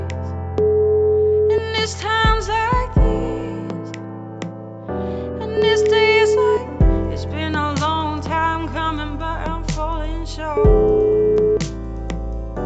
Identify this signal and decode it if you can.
Ringback tone (Europe, 425 Hz), with noise.